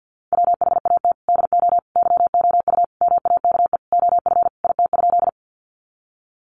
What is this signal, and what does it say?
Morse code, heard as a keyed tone, about 50 wpm, 700 Hz. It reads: WHAT DO YOU MAKE OF IT?
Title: Sentences from Top 100 Words 50wpm